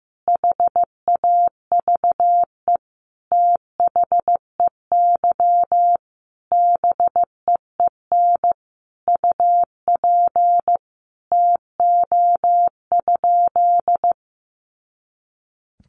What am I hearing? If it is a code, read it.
Morse code, heard as a keyed tone, about 15 wpm, 700 Hz. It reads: HAVE THEY BEEN UP TO?
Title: Sentences from Top 100 Words 15wpm